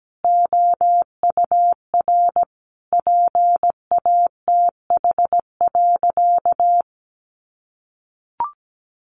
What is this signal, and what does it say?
Morse code, heard as a keyed tone, about 17 wpm, 700 Hz. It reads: OUR PATH.
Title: Sentences from Top 1400 Words 17wpm